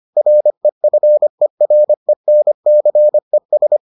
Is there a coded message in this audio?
Yes: Morse code from 0.2 to 3.8 s, heard as a keyed tone, about 25 wpm, 600 Hz. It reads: REFERENCES